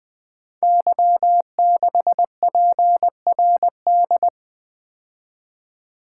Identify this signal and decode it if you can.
Morse code, heard as a keyed tone, about 20 wpm, 700 Hz. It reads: Y6PRD
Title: Code Groups - 5 Letter:Numbers - 20wpm